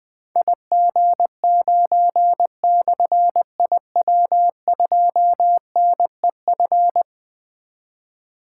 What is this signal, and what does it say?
Morse code, heard as a keyed tone, about 20 wpm, 700 Hz. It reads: IG9/IW2NEF
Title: Difficult Call Signs 20wpm